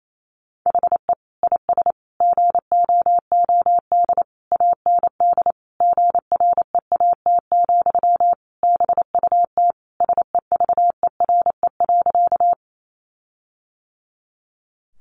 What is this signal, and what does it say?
Morse code, heard as a keyed tone, about 28 wpm, 700 Hz. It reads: HE IS GOOD AND GREAT, BUT SEVERE.
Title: Sentences from Top 1400 Words 28wpm